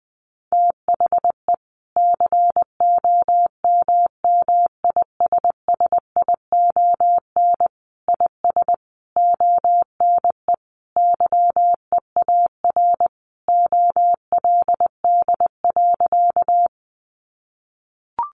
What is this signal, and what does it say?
Morse code, heard as a keyed tone, about 19 wpm, 700 Hz. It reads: THE COMMISSION IS ONE YEAR OLD.
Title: Sentences from Top 1400 Words 20wpm